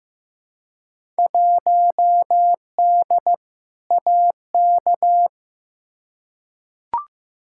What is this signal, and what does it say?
Morse code, heard as a keyed tone, about 15 wpm, 700 Hz. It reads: 1D AK
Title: Field Day Exchanges - 15wpm